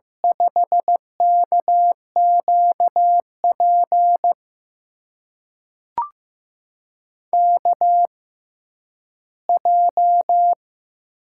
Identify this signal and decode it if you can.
Morse code, heard as a keyed tone, about 15 wpm, spaced as for 10 wpm, 700 Hz. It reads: 5KQP KJ
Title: USA Call Signs - 6 Characters - 6x Character Spacing - 15wpm